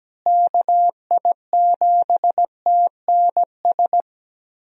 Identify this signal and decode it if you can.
Morse code, heard as a keyed tone, about 17 wpm, 700 Hz. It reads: KI7TNS
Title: USA Call Signs - 6 Characters - Repeated 3x - 17wpm